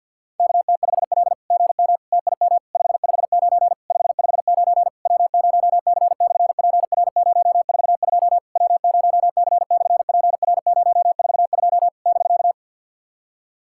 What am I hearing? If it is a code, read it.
Morse code, heard as a keyed tone, about 48 wpm, 690 Hz. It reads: KT5L GM TIM 559 559 W0C/PR042 W0C/PR042 <BK>
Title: SOTA QSO Practice 50wpm